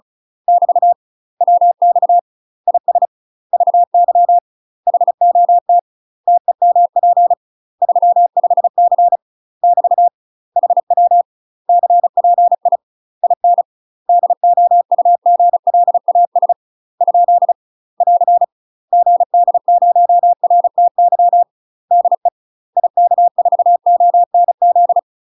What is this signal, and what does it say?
Morse code, heard as a keyed tone, about 34 wpm, 700 Hz. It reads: <BT> WX IS VY HOT TEMP 35C <BT> HW CPI IN DOUGLAS ? <AR> GD0RTY DE IK4ONZ
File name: UK MASTER QSOs TRAINING / 35WPM / UK MASTER QSOs 0003